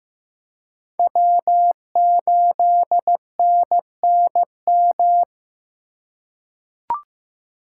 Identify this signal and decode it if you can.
Morse code, heard as a keyed tone, about 15 wpm, 700 Hz. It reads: W8NNM
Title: USA Call Signs - 5 Characters - 4x Character Spacing - 15wpm